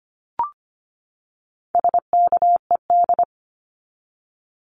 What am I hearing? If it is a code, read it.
Morse code, heard as a keyed tone, about 25 wpm, 700 Hz. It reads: SKED
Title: Abbreviations 25wpm